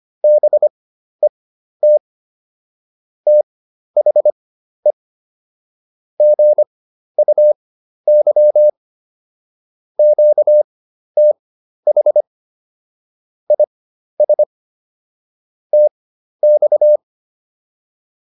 Morse code, heard as a keyed tone, about 25 wpm, spaced as for 12 wpm, 600 Hz. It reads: BET THE GUY QTH IS TX